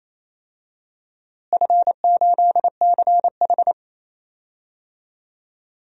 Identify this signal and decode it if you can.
Morse code, heard as a keyed tone, about 28 wpm, 700 Hz. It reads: F8CH
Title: DXCC-100 Call Signs - 4 Characters - Repeated 2x - 28wpm